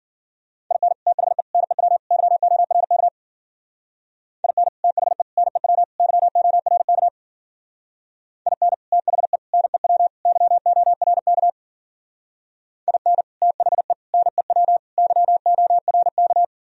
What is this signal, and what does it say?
Morse code, heard as a keyed tone, about 51 wpm, 700 Hz. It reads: IN THE NEW YORK IN THE NEW YORK IN THE NEW YORK IN THE NEW YORK